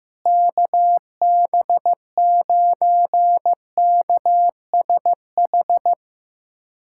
Morse code, heard as a keyed tone, about 15 wpm, 700 Hz. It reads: KB9KSH
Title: USA Callsign DE Callsign - 6 Characters - 15wpm